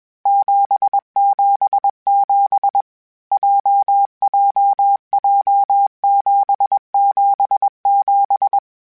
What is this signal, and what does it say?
Morse code, heard as a keyed tone, about 21 wpm, 800 Hz. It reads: 777 JJJ 777
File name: Basic Letter Groups Collective7/JZ17@20